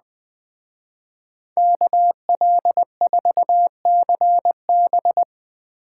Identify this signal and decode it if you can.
Morse code, heard as a keyed tone, about 20 wpm, 700 Hz. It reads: KL4CB